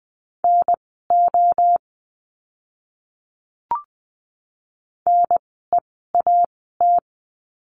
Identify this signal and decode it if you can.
Morse code, heard as a keyed tone, about 20 wpm, spaced as for 14 wpm, 700 Hz. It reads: NO NEAT
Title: 036 Lesson 7f - Letter s - Words 20wpm